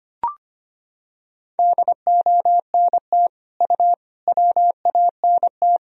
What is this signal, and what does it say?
Morse code, heard as a keyed tone, about 25 wpm, 700 Hz. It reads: DONT U WANT
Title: Sentences from Top 300 Words and Abbreviations 25wpm